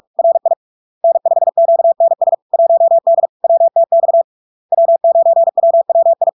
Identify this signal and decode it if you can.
Morse code, heard as a keyed tone, about 46 wpm, 670 Hz. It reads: RI N5QNS 1D WTX W9WWI
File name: LARK-FD-2022-Log-in-CW-45wpm-670-Hz